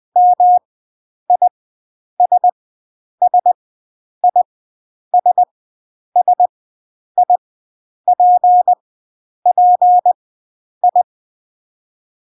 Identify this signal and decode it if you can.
Morse code, heard as a keyed tone, about 20 wpm, 720 Hz. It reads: M I S S I S S I P P I
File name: CW Learning Programs / sounds and personalities of cw and transmitters / 09 - Ms5 - Farnsworth